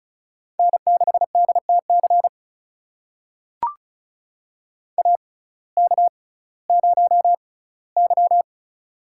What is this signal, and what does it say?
Morse code, heard as a keyed tone, about 34 wpm, spaced as for 19 wpm, 700 Hz. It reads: N6DTC AK0Y
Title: USA Call Signs - 6x Character Spacing - 35wpm